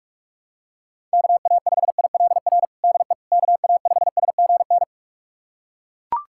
Morse code, heard as a keyed tone, about 41 wpm, 700 Hz. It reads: KA5ILR DE KA5SGN
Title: USA Callsign DE Callsign - 6 Characters - 45wpm